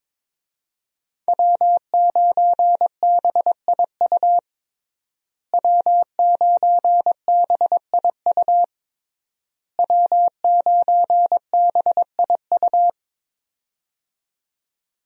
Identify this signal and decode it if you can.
Morse code, heard as a keyed tone, about 22 wpm, 700 Hz. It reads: W9BIU W9BIU W9BIU
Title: USA Call Signs - 5 Characters - Repeated 3x - 22wpm